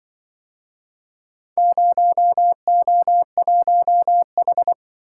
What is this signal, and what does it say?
Morse code, heard as a keyed tone, about 26 wpm, 700 Hz. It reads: 0O1H